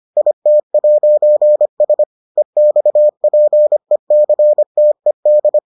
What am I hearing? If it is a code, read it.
Morse code, heard as a keyed tone, about 25 wpm, 600 Hz. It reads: IT'S EXPECTED